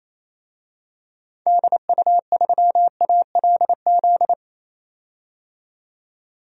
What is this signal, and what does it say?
Morse code, heard as a keyed tone, about 28 wpm, 700 Hz. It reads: DU3ALZ
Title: DXCC-100 Call Signs - 6 Characters - 28wpm